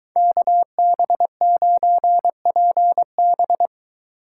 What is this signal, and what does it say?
Morse code, heard as a keyed tone, about 22 wpm, 700 Hz. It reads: KB9PB